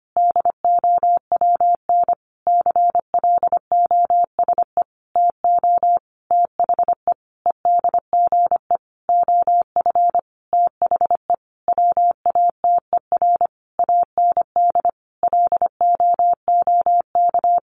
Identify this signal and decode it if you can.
Morse code, heard as a keyed tone, about 25 wpm, 700 Hz. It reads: DOWN CLOSE TO THE EDGE OF THE WATER AND LOOK
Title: Sentences from Top 1300 Words 25wpm